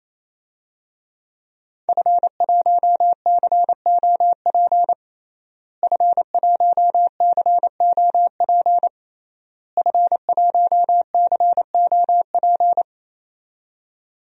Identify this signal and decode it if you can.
Morse code, heard as a keyed tone, about 28 wpm, 690 Hz. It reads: F1COP F1COP F1COP
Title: DXCC-100 Call Signs - 5 Characters - Repeated 3x - 28wpm